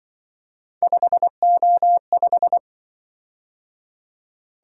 Morse code, heard as a keyed tone, about 25 wpm, 700 Hz. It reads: 5O5
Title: Speed-Racing - Three Letter-Number Combinations - 20wpm